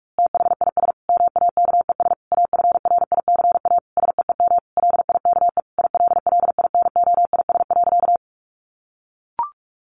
Morse code, heard as a keyed tone, about 44 wpm, 700 Hz. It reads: THIS MAKES AFRICA SEEM LIKE ILLINOIS.
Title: A Sound Of Thunder 45wpm